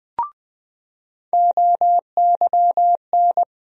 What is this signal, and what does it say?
Morse code, heard as a keyed tone, about 20 wpm, 700 Hz. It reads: OYN
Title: USA Call Sign Postfixes - 3 Letters 20wpm